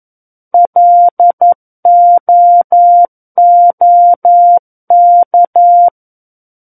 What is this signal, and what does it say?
Morse code, heard as a keyed tone, about 11 wpm, 700 Hz. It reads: LOOK
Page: War of the Worlds - 02-Chapter 2 - 11 WPM